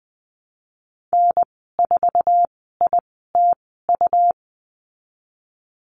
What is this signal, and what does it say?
Morse code, heard as a keyed tone, about 20 wpm, spaced as for 14 wpm, 700 Hz. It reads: N4ITU
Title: 161 Lesson 26g - Letter g - Call Signs 20wpm